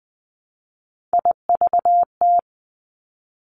Morse code, heard as a keyed tone, about 20 wpm, 700 Hz. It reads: IVT
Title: Three Challenging Letter-Number Combinations 20wpm